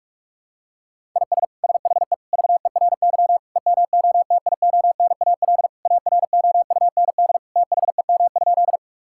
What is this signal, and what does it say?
Morse code, heard as a keyed tone, about 46 wpm, 700 Hz. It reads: IS SHE VERY EMOTIONAL AROUND THEM?
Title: Sentences from Top 1000 Words - Review Part 5 - 45wpm